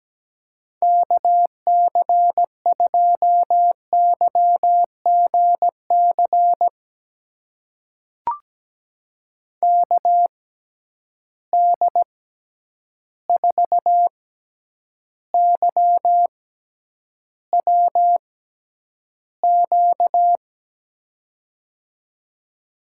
Morse code, heard as a keyed tone, about 17 wpm, spaced as for 9 wpm, 700 Hz. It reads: KC2YGC KD4YWQ
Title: USA Call Signs - 6 Characters - 6x Character Spacing - 17wpm